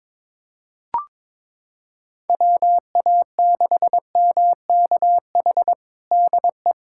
Morse code, heard as a keyed tone, about 22 wpm, 700 Hz. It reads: WA6MKH DE
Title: Practice QSO - Exchange 4 - Ending - 22wpm